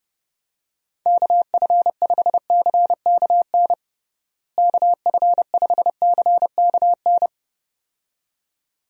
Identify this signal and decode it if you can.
Morse code, heard as a keyed tone, about 32 wpm, 700 Hz. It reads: KF5CKN KF5CKN
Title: USA Call Signs - 6 Characters - Repeated 2x - 30wpm